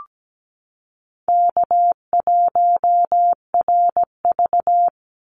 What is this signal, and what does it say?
Morse code, heard as a keyed tone, about 17 wpm, 700 Hz. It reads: K1RV
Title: Easy Call Signs - 17wpm